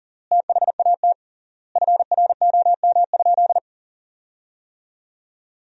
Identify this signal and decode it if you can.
Morse code, heard as a keyed tone, about 40 wpm, 700 Hz. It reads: THAT FROM?
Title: 3x Word-Spacing - Sentences from Top 300 Words - 40wpm